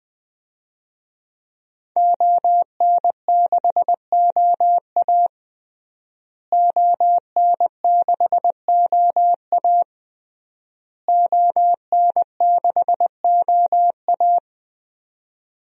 Morse code, heard as a keyed tone, about 21 wpm, 700 Hz. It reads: ON6OA ON6OA ON6OA
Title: DXCC-100 Call Signs - 5 Characters - Repeated 3x - 20wpm